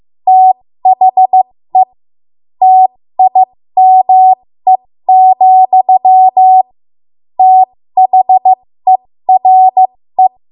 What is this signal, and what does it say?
Morse code, heard as a keyed tone, about 15 wpm, spaced as for 13 wpm, 750 Hz. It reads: THE TIME, THERE
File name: Morse / 13WPM